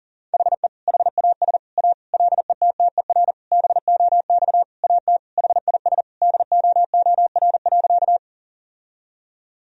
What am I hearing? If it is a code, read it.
Morse code, heard as a keyed tone, about 40 wpm, 700 Hz. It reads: HE HAS A LETTER BOX AT HIS DOOR.
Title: Sentences from Top 1000 Words - Review Part 2 - 40wpm